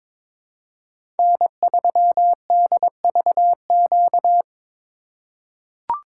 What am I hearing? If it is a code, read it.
Morse code, heard as a keyed tone, about 22 wpm, 700 Hz. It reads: N3DVQ
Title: USA Call Signs - 4x Character Spacing - 22wpm